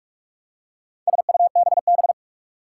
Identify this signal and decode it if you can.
Morse code, heard as a keyed tone, about 46 wpm, 700 Hz. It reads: SUBB
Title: Four Challenging Letter-Number Combinations 45wpm